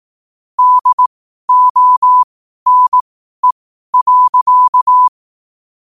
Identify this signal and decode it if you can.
Morse code, heard as a keyed tone, about 18 wpm, spaced as for 12 wpm, 1000 Hz. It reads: DONE.
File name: qotdfiles / 12wpm